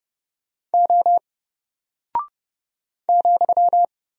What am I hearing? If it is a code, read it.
Morse code, heard as a keyed tone, about 30 wpm, 700 Hz. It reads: O ,
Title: Single Character (Letter:Number:Punctuation:Prosign) 30wpm